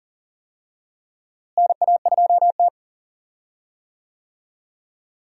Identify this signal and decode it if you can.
Morse code, heard as a keyed tone, about 40 wpm, 700 Hz. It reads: NA2T
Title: USA Call Signs - 4 Characters - Repeated 2x - 40wpm